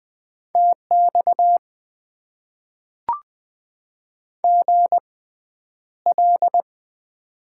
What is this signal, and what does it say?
Morse code, heard as a keyed tone, about 20 wpm, spaced as for 10 wpm, 700 Hz. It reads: TX GL
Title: Top 90 QSO Elements - 6x Character Spacing - 20wpm